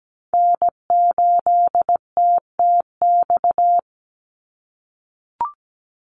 Morse code, heard as a keyed tone, about 17 wpm, 700 Hz. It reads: N8TTX
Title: USA Call Signs 17wpm